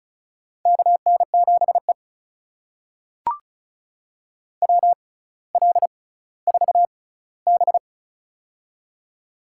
Morse code, heard as a keyed tone, about 36 wpm, spaced as for 18 wpm, 700 Hz. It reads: KN7E WL4B
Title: USA Call Signs - 4 Characters - 6x Character Spacing - 35wpm